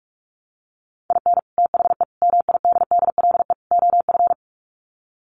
Morse code, heard as a keyed tone, about 44 wpm, 700 Hz. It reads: IN THE MIDDLE OF
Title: Sets of 4 Words 45wpm